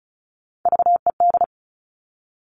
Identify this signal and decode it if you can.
Morse code, heard as a keyed tone, about 36 wpm, 700 Hz. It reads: VED
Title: Top Three Letter Combinations in Words 35wpm